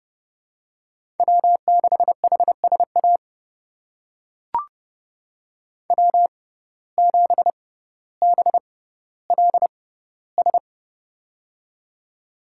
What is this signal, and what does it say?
Morse code, heard as a keyed tone, about 31 wpm, spaced as for 16 wpm, 700 Hz. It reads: W6HSA W7BLS
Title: USA Call Signs - 5 Characters - 6x Character Spacing - 30wpm